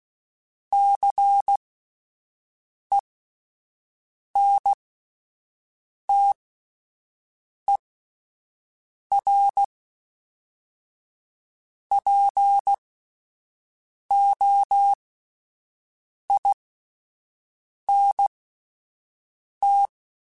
Morse code, heard as a keyed tone, about 16 wpm, spaced as for 5 wpm, 800 Hz. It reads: CENTERPOINT